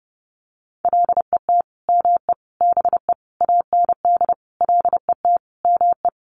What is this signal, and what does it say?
Morse code, heard as a keyed tone, about 30 wpm, 700 Hz. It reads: LET ME BE AND LET ME
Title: Sentences from Top 300 Words 30wpm